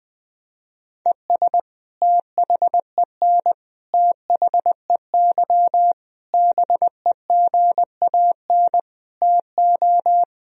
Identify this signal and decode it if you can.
Morse code, heard as a keyed tone, about 21 wpm, 700 Hz. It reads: ES THEN THEY BEGAN TO
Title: Sentences from Top 300 Words and Abbreviations 20wpm